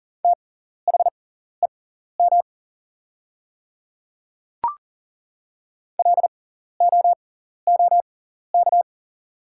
Morse code, heard as a keyed tone, about 40 wpm, spaced as for 14 wpm, 690 Hz. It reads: THEM LOOK
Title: 6x Character Spacing - Top 200 Words - 40wpm